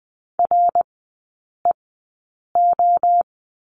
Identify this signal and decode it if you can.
Morse code, heard as a keyed tone, about 20 wpm, spaced as for 8 wpm, 700 Hz. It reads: REO
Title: 060 Lesson 11e - Letter h - Three Characters 20wpm